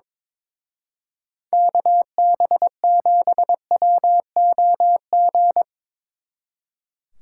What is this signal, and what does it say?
Morse code, heard as a keyed tone, about 24 wpm, 700 Hz. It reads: KB7WOG
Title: Speed Racing - USA 2x3 Call Signs 22wpm